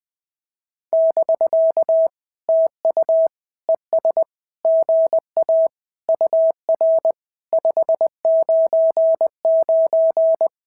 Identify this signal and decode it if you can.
Morse code, heard as a keyed tone, about 20 wpm, 650 Hz. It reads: <BK> TU ES GA UR 599
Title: SOTA QSO Practice 20wpm